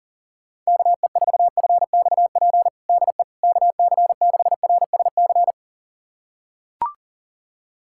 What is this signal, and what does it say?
Morse code, heard as a keyed tone, about 40 wpm, 700 Hz. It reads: KE4FXP DE KC6RSC
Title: USA Callsign DE Callsign - 6 Characters - 40wpm